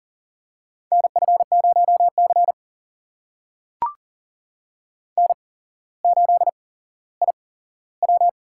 Morse code, heard as a keyed tone, about 42 wpm, spaced as for 17 wpm, 700 Hz. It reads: NF0C N8IW
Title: USA Call Signs - 4 Characters - 8x Character Spacing - 40wpm